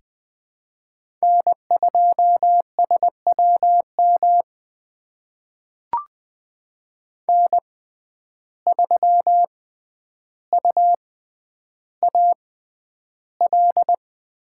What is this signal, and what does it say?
Morse code, heard as a keyed tone, about 20 wpm, spaced as for 10 wpm, 700 Hz. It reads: N2SWM N3UAL
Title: USA Call Signs - 5 Characters - 6x Character Spacing - 20wpm